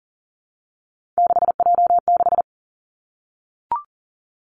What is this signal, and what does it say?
Morse code, heard as a keyed tone, about 38 wpm, 700 Hz. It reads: 6J6
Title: Three Challenging Letter-Number Combinations 40wpm